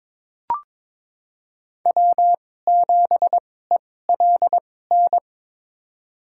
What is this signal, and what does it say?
Morse code, heard as a keyed tone, about 22 wpm, spaced as for 16 wpm, 700 Hz. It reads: W7ELN
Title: USA Call Signs - 5 Characters - 2x Character Spacing - 22wpm